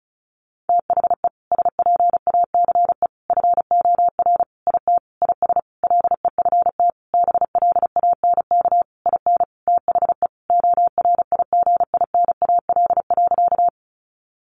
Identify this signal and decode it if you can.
Morse code, heard as a keyed tone, about 34 wpm, 700 Hz. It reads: THE SPACE FOR IT IS LEFT BLANK IN THE ORIGINAL.
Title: Sentences from Top 1500 Words 35wpm